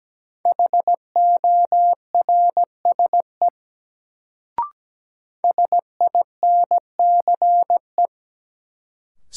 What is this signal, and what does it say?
Morse code, heard as a keyed tone, about 17 wpm, 700 Hz. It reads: HORSE SINCE
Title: Sentences from Top 300 Words - New Word Drill - 17wpm